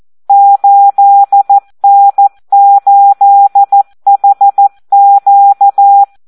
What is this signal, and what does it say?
Morse code, heard as a keyed tone, about 14 wpm, 800 Hz. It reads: 8N8HQ
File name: Audio practica CW / SD 1 14